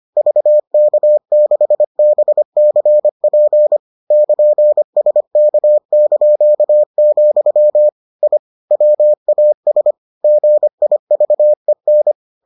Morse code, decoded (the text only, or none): VK6BCP <KN>SK), I WAS GIVEN